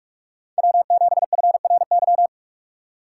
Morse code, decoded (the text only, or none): W7FRY